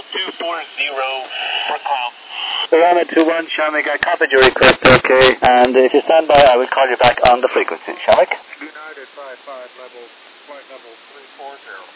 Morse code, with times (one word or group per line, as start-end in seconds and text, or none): none